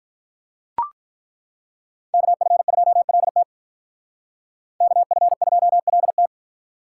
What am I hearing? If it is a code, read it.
Morse code, heard as a keyed tone, about 43 wpm, 700 Hz. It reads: KF2LT KF2LT
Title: Speed Racing - USA 2x2 Call Signs 35wpm